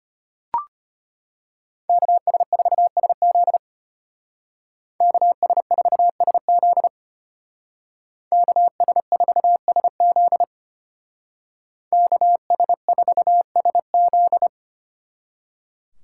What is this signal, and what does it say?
Morse code, heard as a keyed tone, about 33 wpm, 700 Hz. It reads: KS4SZ KS4SZ KS4SZ KS4SZ